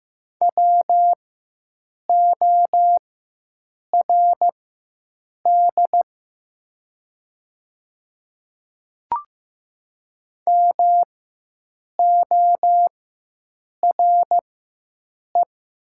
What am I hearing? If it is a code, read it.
Morse code, heard as a keyed tone, about 15 wpm, spaced as for 7 wpm, 700 Hz. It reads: WORD MORE
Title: 4x Character Spacing - Top 100 Words - 15wpm